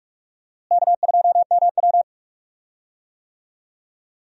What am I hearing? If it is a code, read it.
Morse code, heard as a keyed tone, about 44 wpm, 700 Hz. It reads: K2MW
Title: USA Call Signs - 4 Characters - Repeated 2x - 45wpm